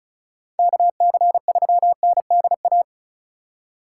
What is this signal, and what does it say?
Morse code, heard as a keyed tone, about 47 wpm, spaced as for 41 wpm, 700 Hz. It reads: KC3NDA
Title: Speed Racing - USA 2x3 Call Signs 35wpm